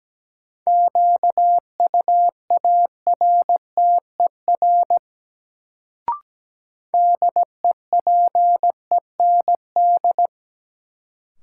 Morse code, heard as a keyed tone, about 17 wpm, 700 Hz. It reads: QUARTER DEPEND
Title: Sentences from Top 1300 Words - New Word Drill Alt - 17wpm